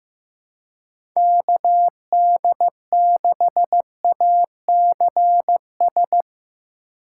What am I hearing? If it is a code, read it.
Morse code, heard as a keyed tone, about 15 wpm, 700 Hz. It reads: KD6ACS
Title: USA Call Signs - 6 Characters - 2x Character Spacing - 15wpm